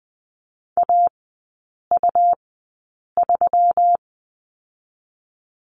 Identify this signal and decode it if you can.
Morse code, heard as a keyed tone, about 20 wpm, 700 Hz. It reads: A U 3
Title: 121 Lesson 20e - Number 3 - Three Characters 20wpm